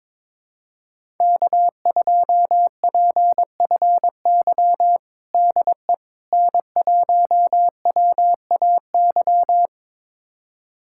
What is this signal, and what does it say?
Morse code, heard as a keyed tone, about 22 wpm, 700 Hz. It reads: K2PFY DE N1WAY
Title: USA Callsign DE Callsign 22wpm